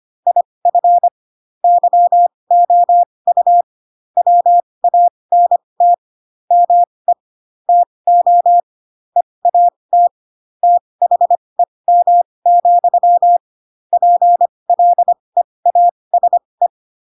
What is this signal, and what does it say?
Morse code, heard as a keyed tone, about 25 wpm, spaced as for 20 wpm, 700 Hz. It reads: IF YOU WANT ME TO EAT THEM, PLEASE
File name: Pinocchio / Pinocchio ch7 15 25 ik1hge